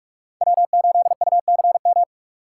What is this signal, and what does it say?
Morse code, heard as a keyed tone, about 46 wpm, 700 Hz. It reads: W8UCM
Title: USA Call Signs - 5 Characters - 2x Character Spacing - 45wpm